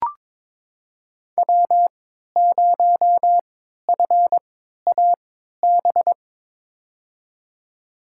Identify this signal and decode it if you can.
Morse code, heard as a keyed tone, about 22 wpm, spaced as for 13 wpm, 700 Hz. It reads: W0FAB
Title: USA Call Signs - 5 Characters - 3x Character Spacing - 22wpm